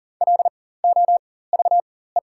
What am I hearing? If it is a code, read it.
Morse code, heard as a keyed tone, about 40 wpm, spaced as for 19 wpm, 700 Hz. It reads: LOVE
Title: 4x Character Spacing - Top 500 Words - 40wpm